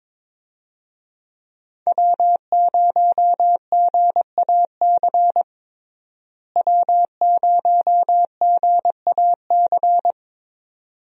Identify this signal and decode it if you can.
Morse code, heard as a keyed tone, about 22 wpm, 700 Hz. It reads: W0GAC W0GAC
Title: USA Call Signs - 5 Characters - Repeated 2x - 22wpm